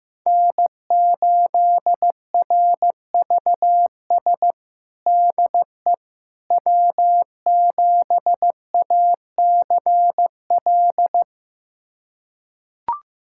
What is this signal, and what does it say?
Morse code, heard as a keyed tone, about 15 wpm, 700 Hz. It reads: N8RVS DE W7ACL
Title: USA Callsign DE Callsign - 5 Characters - 15wpm